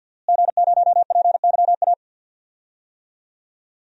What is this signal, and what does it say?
Morse code, heard as a keyed tone, about 47 wpm, 700 Hz. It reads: G0PYA
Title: DXCC-100 Call Signs - 5 Characters - Repeated 3x - 50wpm